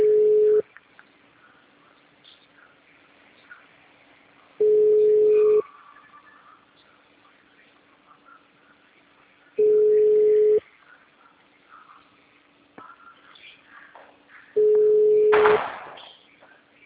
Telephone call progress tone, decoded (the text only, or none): ringback tone